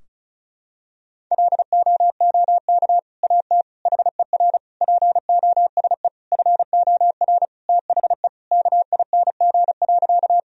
Morse code, heard as a keyed tone, about 34 wpm, 700 Hz. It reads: LOOK AT HER POSE FOR THE KING.
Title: Sentences from Top 1000 Words - Review Part 2 - 35wpm